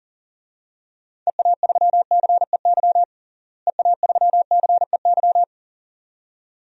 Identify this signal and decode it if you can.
Morse code, heard as a keyed tone, about 40 wpm, 700 Hz. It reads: EA3CEY EA3CEY
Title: DXCC-100 Call Signs - 6 Characters - Repeated 2x - 40wpm